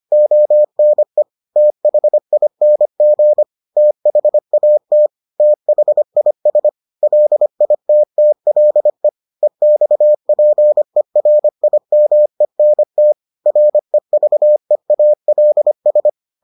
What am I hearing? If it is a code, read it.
Morse code, heard as a keyed tone, about 25 wpm, 600 Hz. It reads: ONE THING THAT THIS LITTLE EXPERIMENT REVEALS